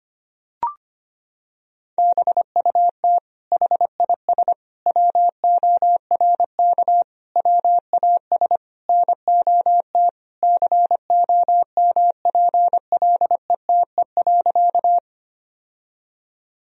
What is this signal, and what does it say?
Morse code, heard as a keyed tone, about 25 wpm, 700 Hz. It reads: BUT HIS WORK WAS NOT COMPLETE.
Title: Sentences from Top 1000 Words - Review Part 2 - 25wpm